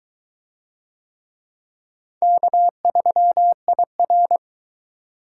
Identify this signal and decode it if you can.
Morse code, heard as a keyed tone, about 21 wpm, 700 Hz. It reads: K3IR